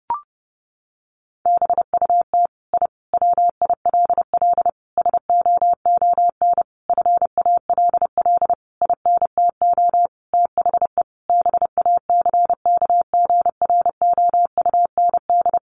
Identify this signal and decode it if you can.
Morse code, heard as a keyed tone, about 30 wpm, 700 Hz. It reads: BUT I WILL SOON FALL INTO THE BACKGROUND
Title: Sentences from Top 1300 Words 30wpm